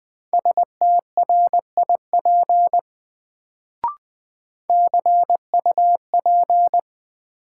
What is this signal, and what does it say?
Morse code, heard as a keyed tone, about 21 wpm, 700 Hz. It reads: STRIP CUP
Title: Sentences from Top 1600 Words - New Word Drill - 20wpm